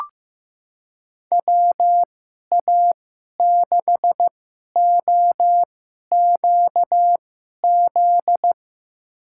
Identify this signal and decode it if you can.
Morse code, heard as a keyed tone, about 15 wpm, spaced as for 11 wpm, 700 Hz. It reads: WA6OQZ
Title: USA Call Signs - 6 Characters - 2x Character Spacing - 15wpm